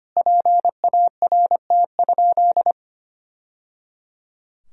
Morse code, heard as a keyed tone, about 25 wpm, 700 Hz. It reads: PART?
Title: Sentences from Top 1000 Words - Review Part 4 - 25wpm